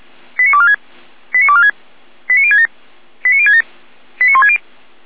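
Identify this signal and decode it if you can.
Five-tone selective call (ZVEI), heard as quick runs of five tones, tones of about 70 ms. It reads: 89246 89246 89067 89067 89160
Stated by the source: Five-tone ZVEI selective call received by SDR.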